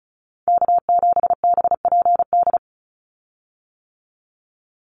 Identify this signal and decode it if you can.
Morse code, heard as a keyed tone, about 36 wpm, 700 Hz. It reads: K7BPD
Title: USA Call Signs 35wpm